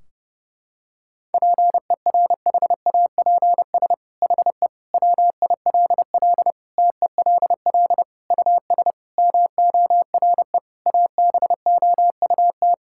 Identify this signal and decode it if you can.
Morse code, heard as a keyed tone, about 30 wpm, 700 Hz. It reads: PERHAPS HE WILL TELL US MORE ABOUT
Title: Sentences from Top 1600 Words 30wpm